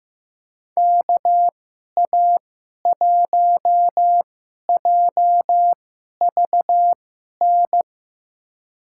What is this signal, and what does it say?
Morse code, heard as a keyed tone, about 15 wpm, spaced as for 11 wpm, 700 Hz. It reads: KA1JVN
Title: USA Call Signs - 6 Characters - 2x Character Spacing - 15wpm